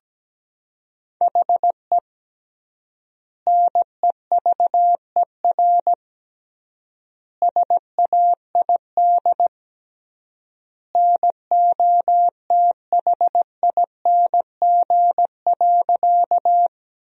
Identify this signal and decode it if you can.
Morse code, heard as a keyed tone, about 17 wpm, 700 Hz. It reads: HE NEVER SAID NOTHING.
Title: 3x Word-Spacing - Sentences from Top 500 Words - 17wpm